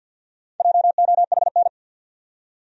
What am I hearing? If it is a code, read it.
Morse code, heard as a keyed tone, about 50 wpm, 700 Hz. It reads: JOHN
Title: USA Common Names - Max 5 Letters - 50wpm